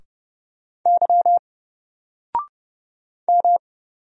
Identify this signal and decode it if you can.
Morse code, heard as a keyed tone, about 31 wpm, 700 Hz. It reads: Y M
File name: Single-Letter-Number-30wpm